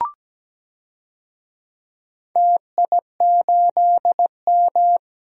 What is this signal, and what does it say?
Morse code, heard as a keyed tone, about 17 wpm, 700 Hz. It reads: TI8M
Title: DXCC-100 Call Signs - 4 Characters - Repeated 2x - 17wpm